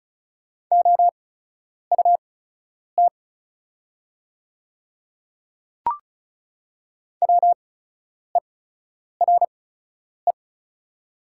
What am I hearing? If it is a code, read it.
Morse code, heard as a keyed tone, about 36 wpm, spaced as for 10 wpm, 690 Hz. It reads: OUT WERE
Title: 8x Character Spacing - Top 200 Words - 35wpm